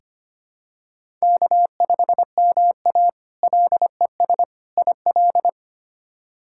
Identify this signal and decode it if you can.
Morse code, heard as a keyed tone, about 25 wpm, 700 Hz. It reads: K5MA LES IL
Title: K1USN SST QSO Practice - 25wpm